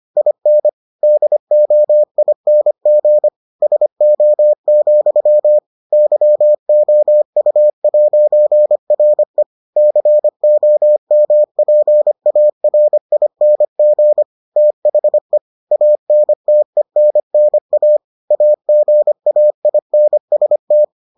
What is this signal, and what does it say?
Morse code, heard as a keyed tone, about 25 wpm, 600 Hz. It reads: IN DOING SO, YOU'RE COMPARING THE ANTENNA AGAINST